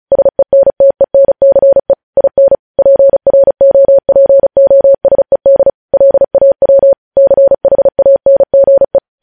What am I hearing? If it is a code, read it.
Morse code, heard as a keyed tone, about 36 wpm, 550 Hz. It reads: SENTENCE IN PROPOSED LAW CHANGE
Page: News headlines in Morse code at 35 wpm